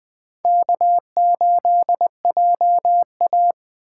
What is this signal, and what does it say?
Morse code, heard as a keyed tone, about 20 wpm, 700 Hz. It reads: K8JA